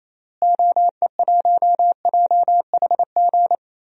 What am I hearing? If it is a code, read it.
Morse code, heard as a keyed tone, about 27 wpm, 700 Hz. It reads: OE1JHG
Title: DXCC-100 Call Signs - 6 Characters - Repeated 3x - 28wpm